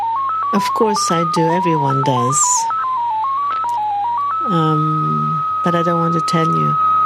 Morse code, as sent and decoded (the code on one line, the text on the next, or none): none